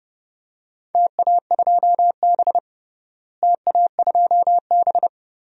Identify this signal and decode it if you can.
Morse code, heard as a keyed tone, about 30 wpm, 700 Hz. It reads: TA2B TA2B
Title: DXCC-100 Call Signs - 4 Characters - Repeated 2x - 30wpm